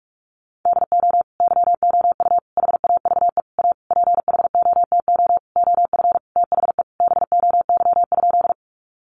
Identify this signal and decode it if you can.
Morse code, heard as a keyed tone, about 46 wpm, 700 Hz. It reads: DO YOU HAVE A PHOTO OF THE BOY?
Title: Sentences from Top 1200 Words 45wpm